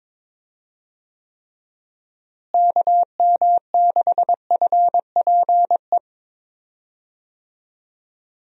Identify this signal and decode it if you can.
Morse code, heard as a keyed tone, about 22 wpm, 700 Hz. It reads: KM6FPE
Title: USA Call Signs - 6 Characters - Repeated 2x - 22wpm